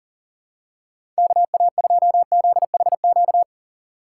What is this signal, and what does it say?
Morse code, heard as a keyed tone, about 37 wpm, 700 Hz. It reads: KA2ZHQ
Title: USA Call Signs - 6 Characters - 6x Character Spacing - 40wpm